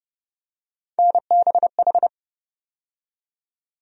Morse code, heard as a keyed tone, about 30 wpm, 700 Hz. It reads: NBH
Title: USA Call Sign Postfixes - 3 Letters 30wpm